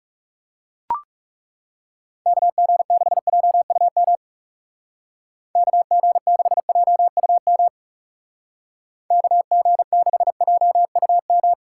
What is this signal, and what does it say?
Morse code, heard as a keyed tone, about 36 wpm, 700 Hz. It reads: KG6JUM KG6JUM KG6JUM
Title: Speed Racing - USA 2x3 Call Signs 30wpm